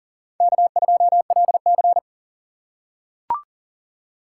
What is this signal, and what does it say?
Morse code, heard as a keyed tone, about 40 wpm, 700 Hz. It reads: K2LC
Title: USA Call Signs - 4 Characters - 2x Character Spacing - 40wpm